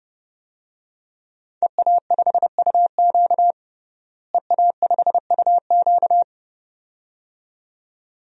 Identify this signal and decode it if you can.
Morse code, heard as a keyed tone, about 30 wpm, 700 Hz. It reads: EA5UQ EA5UQ
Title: DXCC-100 Call Signs - 5 Characters - Repeated 2x - 30wpm